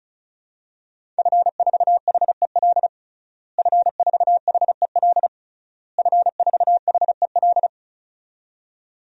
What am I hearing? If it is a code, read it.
Morse code, heard as a keyed tone, about 34 wpm, 700 Hz. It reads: F4HEL F4HEL F4HEL
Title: DXCC-100 Call Signs - 5 Characters - Repeated 3x - 35wpm